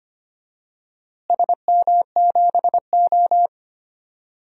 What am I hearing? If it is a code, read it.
Morse code, heard as a keyed tone, about 25 wpm, 700 Hz. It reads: SM7O
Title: DXCC-100 Call Signs - 4 Characters - Repeated 3x - 25wpm